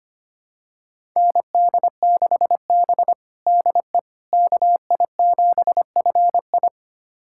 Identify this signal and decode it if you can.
Morse code, heard as a keyed tone, about 25 wpm, 700 Hz. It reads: ND6B DE KI7FI
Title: USA Callsign DE Callsign 25wpm